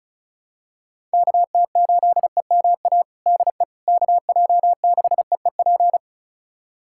Morse code, heard as a keyed tone, about 36 wpm, 700 Hz. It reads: KT8EMA DE KJ6EEP
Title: USA Callsign DE Callsign 35wpm